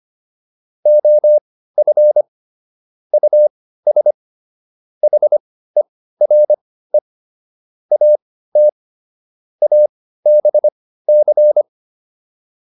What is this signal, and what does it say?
Morse code, heard as a keyed tone, about 25 wpm, spaced as for 15 wpm, 600 Hz. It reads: OF US HERE AT ABC